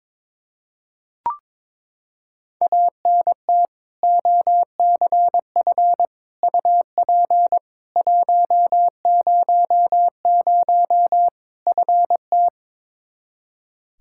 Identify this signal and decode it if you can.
Morse code, heard as a keyed tone, about 22 wpm, 700 Hz. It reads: ANT OCF UP 100 FT
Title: K1USN SST QSO Practice 22wpm